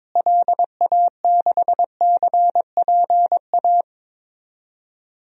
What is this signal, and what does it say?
Morse code, heard as a keyed tone, about 22 wpm, 700 Hz. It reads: LA6CPA
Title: DXCC-100 Call Signs - 6 Characters - Repeated 2x - 22wpm